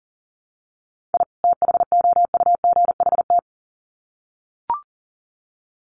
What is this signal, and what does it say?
Morse code, heard as a keyed tone, about 40 wpm, 700 Hz. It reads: I THOUGHT
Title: Sets of 2 Words 40wpm